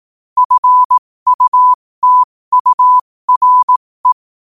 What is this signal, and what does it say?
Morse code, heard as a keyed tone, about 18 wpm, spaced as for 15 wpm, 1000 Hz. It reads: FUTURE
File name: qotdfiles / 15wpm